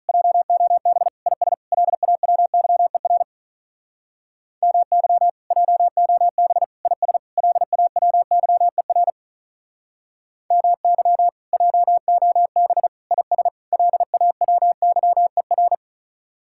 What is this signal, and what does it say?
Morse code, heard as a keyed tone, about 47 wpm, 700 Hz. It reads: JOB IS LAWYER MY JOB IS LAWYER MY JOB IS LAWYER